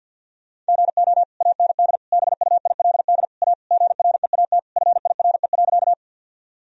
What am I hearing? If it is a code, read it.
Morse code, heard as a keyed tone, about 50 wpm, 700 Hz. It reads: GO AND BUILD A GREAT FIRE.
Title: Sentences from Top 1000 Words - Review Part 1 - 50wpm